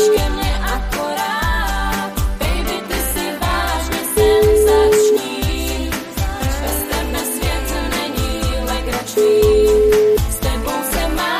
Ringback tone (Europe, 425 Hz), with noise.